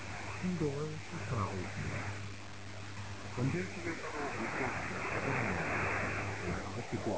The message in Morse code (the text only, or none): none